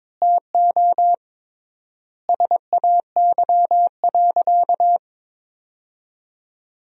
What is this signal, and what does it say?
Morse code, heard as a keyed tone, about 22 wpm, 700 Hz. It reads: TO SAY.
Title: 3x Word-Spacing - Sentences from Top 500 Words - 22wpm